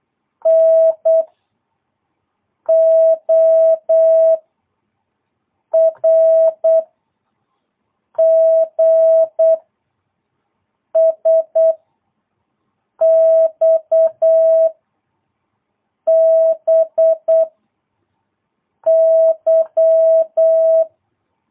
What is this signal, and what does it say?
Morse code, heard as a keyed tone, about 8 wpm, 660 Hz. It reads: N O R G S X B Y
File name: jh8cbh / 2022cw